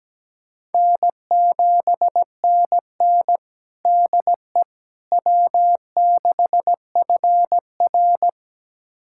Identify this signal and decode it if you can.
Morse code, heard as a keyed tone, about 17 wpm, 700 Hz. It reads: N7NN DE W6FR
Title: USA Callsign DE Callsign - 4 Characters - 17wpm